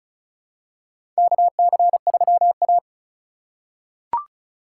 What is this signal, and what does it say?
Morse code, heard as a keyed tone, about 36 wpm, 690 Hz. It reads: KC3A
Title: USA Call Signs - 4 Characters - 3x Character Spacing - 35wpm